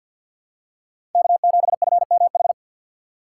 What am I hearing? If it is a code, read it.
Morse code, heard as a keyed tone, about 47 wpm, 700 Hz. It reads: K7FMH
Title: USA Call Signs - 5 Characters - 2x Character Spacing - 50wpm